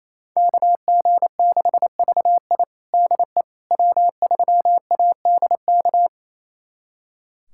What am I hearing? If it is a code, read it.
Morse code, heard as a keyed tone, about 28 wpm, 700 Hz. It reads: KG6VI DE W3ADK
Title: USA Callsign DE Callsign - 5 Characters - 28wpm